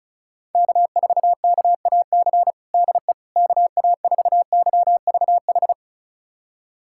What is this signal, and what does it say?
Morse code, heard as a keyed tone, about 35 wpm, 700 Hz. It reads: K4KAC DE KA4YVH
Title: USA Callsign DE Callsign 35wpm